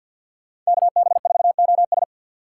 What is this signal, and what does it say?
Morse code, heard as a keyed tone, about 63 wpm, 700 Hz. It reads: KB4OS